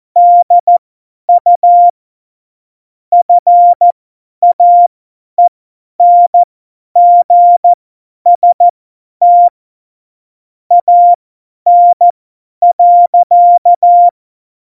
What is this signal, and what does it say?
Morse code, heard as a keyed tone, about 14 wpm, spaced as for 10 wpm, 700 Hz. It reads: DU FAENGST AN.
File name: Morsekurs / Witze / Witz 23. April 2019, 50 F70